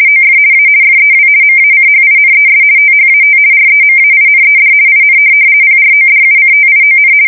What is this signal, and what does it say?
RTTY, about 50 baud. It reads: OF A FEW MACHINES THAT REMAINED IN PRODUCTION
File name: fsk170-50B / teletype-corp